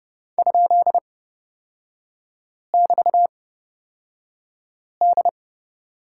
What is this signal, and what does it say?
Morse code, heard as a keyed tone, about 29 wpm, spaced as for 5 wpm, 700 Hz. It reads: ?<BT>D
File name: Lesson 38e - Letter q - Three Characters 30wpm